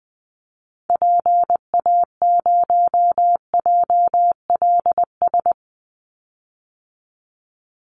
Morse code, heard as a keyed tone, about 20 wpm, 700 Hz. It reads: PA0JLS
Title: Call Signs Repeated 2x - 20wpm